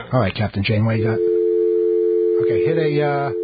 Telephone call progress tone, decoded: dial tone